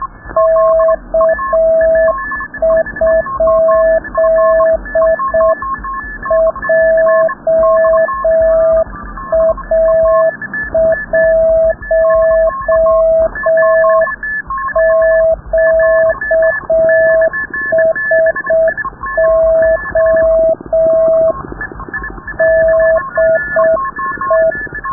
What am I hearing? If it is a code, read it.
Morse code, heard as a keyed tone, about 6 wpm, 650 Hz. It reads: K? JA1QSO DE
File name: rubberst01 30 noise